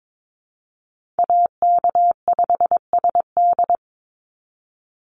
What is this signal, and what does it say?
Morse code, heard as a keyed tone, about 22 wpm, 700 Hz. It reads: AK5SD
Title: NA SOTA Chaser Callsigns 22wpm